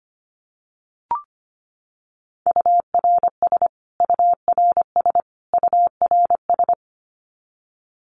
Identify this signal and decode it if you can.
Morse code, heard as a keyed tone, about 25 wpm, 700 Hz. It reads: URS URS URS
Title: Abbreviations Repeated 3x - 25wpm